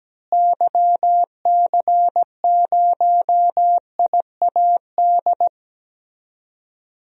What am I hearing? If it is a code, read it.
Morse code, heard as a keyed tone, about 17 wpm, 700 Hz. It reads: YC0IAD
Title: DXCC-100 Call Signs - 6 Characters - Repeated 2x - 17wpm